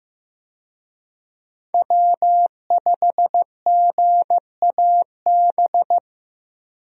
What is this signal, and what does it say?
Morse code, heard as a keyed tone, about 15 wpm, 700 Hz. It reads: W5GAB
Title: USA Call Signs - 5 Characters - Repeated 2x - 15wpm